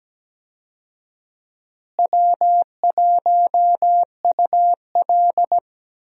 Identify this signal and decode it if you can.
Morse code, heard as a keyed tone, about 17 wpm, 700 Hz. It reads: W1UL
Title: USA Call Signs - 4 Characters - Repeated 2x - 17wpm